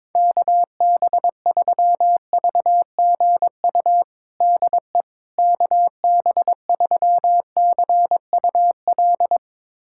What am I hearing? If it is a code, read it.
Morse code, heard as a keyed tone, about 22 wpm, 700 Hz. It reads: KB3VGU DE KB3CUL
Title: USA Callsign DE Callsign 22wpm